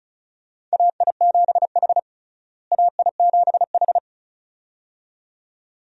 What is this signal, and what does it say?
Morse code, heard as a keyed tone, about 36 wpm, 700 Hz. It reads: AI7H AI7H
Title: USA Call Signs - 4 Characters - Repeated 2x - 35wpm